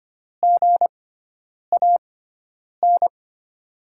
Morse code, heard as a keyed tone, about 25 wpm, spaced as for 9 wpm, 700 Hz. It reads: GAN
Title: USA Call Signs - 6x Character Spacing - 25wpm